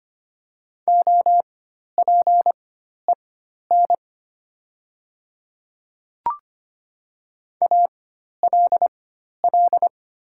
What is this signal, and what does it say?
Morse code, heard as a keyed tone, about 25 wpm, spaced as for 12 wpm, 700 Hz. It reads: OPEN ALL
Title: 4x Character Spacing - Top 300 Words - 25wpm